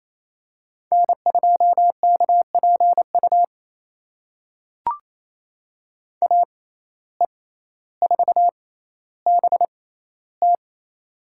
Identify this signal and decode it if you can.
Morse code, heard as a keyed tone, about 28 wpm, spaced as for 14 wpm, 700 Hz. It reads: N2KPU AE4BT
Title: USA Call Signs - 5 Characters - 6x Character Spacing - 28wpm